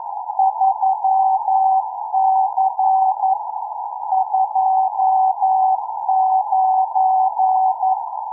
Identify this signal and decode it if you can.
Morse code, heard as a keyed tone, about 11 wpm, 780 Hz. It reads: 3C 29